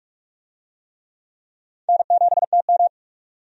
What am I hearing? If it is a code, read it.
Morse code, heard as a keyed tone, about 46 wpm, 700 Hz. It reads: N7TM
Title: USA Call Signs - 4 Characters - 45wpm